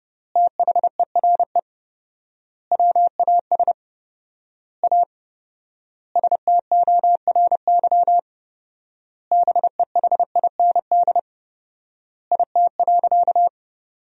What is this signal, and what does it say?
Morse code, heard as a keyed tone, about 30 wpm, 700 Hz. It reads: THERE WAS A STORY BEHIND IT.
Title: 4x Word-Spacing - Sentences from Top 500 Words - 30wpm